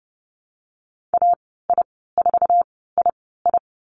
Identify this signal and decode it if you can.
Morse code, heard as a keyed tone, about 30 wpm, spaced as for 17 wpm, 700 Hz. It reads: AI4II
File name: Lesson 14g - Number 2 - Call Signs 30wpm